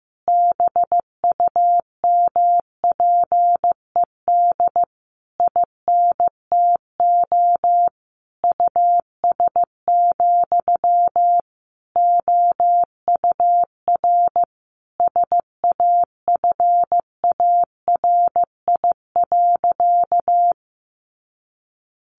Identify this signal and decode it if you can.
Morse code, heard as a keyed tone, about 15 wpm, 700 Hz. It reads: BUMPED INTO US, OUR SAFARI.
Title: A Sound Of Thunder 15wpm